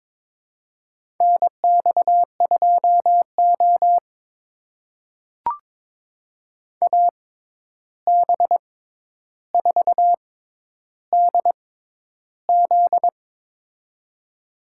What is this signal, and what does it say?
Morse code, heard as a keyed tone, about 22 wpm, spaced as for 11 wpm, 700 Hz. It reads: NX2O AB4DZ
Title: USA Call Signs - 6x Character Spacing - 22wpm